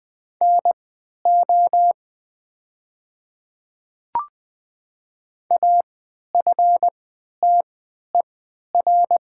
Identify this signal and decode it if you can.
Morse code, heard as a keyed tone, about 20 wpm, spaced as for 11 wpm, 700 Hz. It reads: NO AFTER
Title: 3x Character Spacing - Top 200 Words - 20wpm